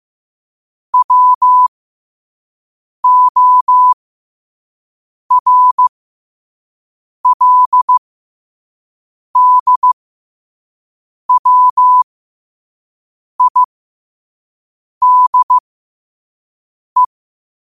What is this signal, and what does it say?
Morse code, heard as a keyed tone, about 15 wpm, spaced as for 5 wpm, 1000 Hz. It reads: WORLDWIDE
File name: qotdfiles / 5wpm